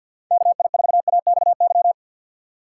Morse code, heard as a keyed tone, about 51 wpm, 700 Hz. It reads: KI4AXY
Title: USA Call Signs - 2x Character Spacing - 50wpm